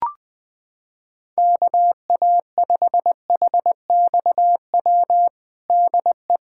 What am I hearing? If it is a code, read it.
Morse code, heard as a keyed tone, about 20 wpm, 700 Hz. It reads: KA5HXW DE
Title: USA Callsign DE Callsign 20wpm